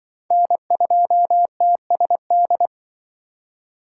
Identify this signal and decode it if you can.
Morse code, heard as a keyed tone, about 26 wpm, 700 Hz. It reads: N2TSD